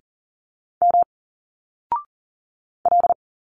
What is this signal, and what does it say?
Morse code, heard as a keyed tone, about 39 wpm, 700 Hz. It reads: M L
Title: Single Letters 40wpm